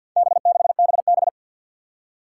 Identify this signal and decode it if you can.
Morse code, heard as a keyed tone, about 51 wpm, 700 Hz. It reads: B6BB
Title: Four Challenging Letter-Number Combinations 50wpm